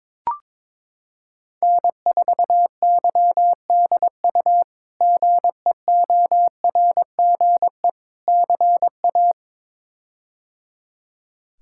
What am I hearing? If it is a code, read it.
Morse code, heard as a keyed tone, about 21 wpm, 700 Hz. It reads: N4YDU GEORGE CA
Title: K1USN SST QSO Practice - 22wpm